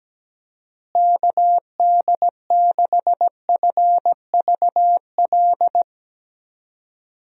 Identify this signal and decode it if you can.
Morse code, heard as a keyed tone, about 17 wpm, 700 Hz. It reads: KD6FVL
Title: USA Call Signs - 6 Characters - Repeated 2x - 17wpm